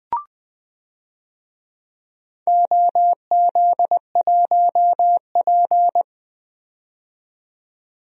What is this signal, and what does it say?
Morse code, heard as a keyed tone, about 20 wpm, 700 Hz. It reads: OZ1P
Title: DXCC-100 Call Signs - 4 Characters - 20wpm